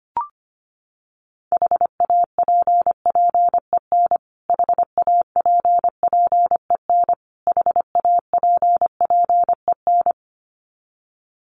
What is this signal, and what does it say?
Morse code, heard as a keyed tone, about 25 wpm, 700 Hz. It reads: HAPPEN HAPPEN HAPPEN
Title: Top 500 Words Repeated 25wpm